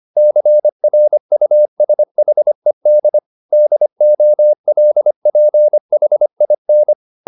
Morse code, heard as a keyed tone, about 25 wpm, 600 Hz. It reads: CRUSHED DOLPHIN